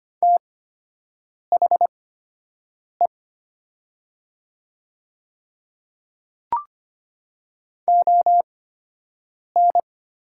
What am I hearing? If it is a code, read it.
Morse code, heard as a keyed tone, about 25 wpm, spaced as for 7 wpm, 700 Hz. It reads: THE ON
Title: 8x Character Spacing - Top 200 Words - 25wpm